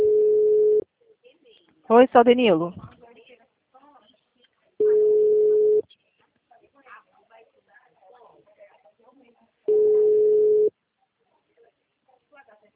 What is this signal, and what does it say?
Ringback tone (Europe, 425 Hz), with speech.